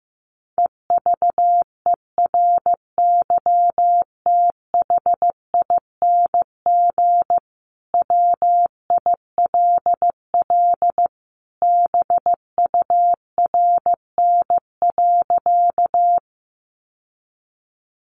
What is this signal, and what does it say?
Morse code, heard as a keyed tone, about 15 wpm, 700 Hz. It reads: EVERYTHING WILL BURN.
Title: Sentences from Top 1300 Words 15wpm